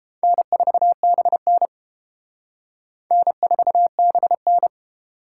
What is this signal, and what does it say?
Morse code, heard as a keyed tone, about 30 wpm, 700 Hz. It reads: N4BN N4BN